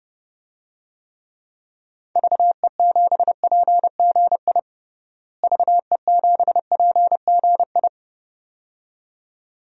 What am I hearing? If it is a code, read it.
Morse code, heard as a keyed tone, about 30 wpm, 700 Hz. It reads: VE7PGI VE7PGI
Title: DXCC-100 Call Signs - 6 Characters - Repeated 2x - 30wpm